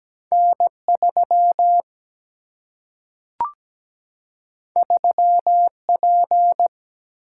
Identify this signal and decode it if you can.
Morse code, heard as a keyed tone, about 17 wpm, 700 Hz. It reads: N3 3P
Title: Single-Letter Single-Number 17wpm